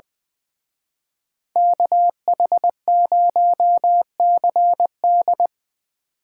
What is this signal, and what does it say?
Morse code, heard as a keyed tone, about 24 wpm, 700 Hz. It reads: KH0CD
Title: Speed Racing - USA 2x2 Call Signs 20wpm